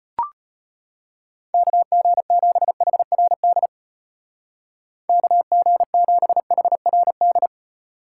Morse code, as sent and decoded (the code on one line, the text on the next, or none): -.- --. --... .... .-. -.. / -.- --. --... .... .-. -..
KG7HRD KG7HRD